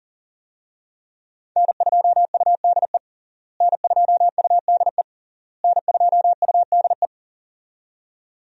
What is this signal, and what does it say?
Morse code, heard as a keyed tone, about 40 wpm, 700 Hz. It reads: N2UDE N2UDE N2UDE
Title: USA Call Signs - 5 Characters - Repeated 3x - 40wpm